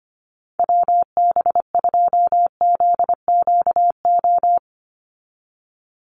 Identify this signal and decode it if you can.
Morse code, heard as a keyed tone, about 25 wpm, 700 Hz. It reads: WB2ZQO
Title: USA Call Signs 25wpm